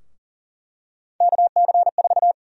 Morse code, heard as a keyed tone, about 39 wpm, 700 Hz. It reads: KC4
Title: USA Call Sign Prefixes - Repeated 2x - 40wpm